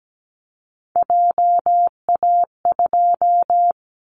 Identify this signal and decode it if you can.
Morse code, heard as a keyed tone, about 17 wpm, 700 Hz. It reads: JA2
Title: Callsign Prefixes for Top 50 DXCC - 17wpm